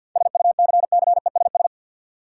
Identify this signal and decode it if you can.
Morse code, heard as a keyed tone, about 46 wpm, 700 Hz. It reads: SUCCESS
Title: Sentences from Top 900 Words - New Word Drill - 50wpm